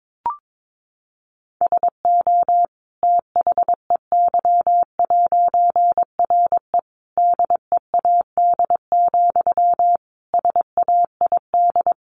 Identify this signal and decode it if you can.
Morse code, heard as a keyed tone, about 22 wpm, 700 Hz. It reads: SO THEY'RE DEAD, SAID
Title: A Sound Of Thunder 22wpm